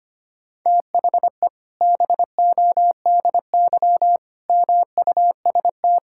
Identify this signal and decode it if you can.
Morse code, heard as a keyed tone, about 25 wpm, 700 Hz. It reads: THE BODY MUST
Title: Sentences from Top 1000 Words - Review Part 5 - 25wpm